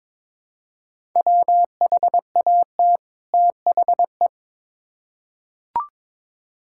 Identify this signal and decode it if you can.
Morse code, heard as a keyed tone, about 21 wpm, 700 Hz. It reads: WHAT THE
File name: Build A Phrase - 2-Word Phrases - 22wpm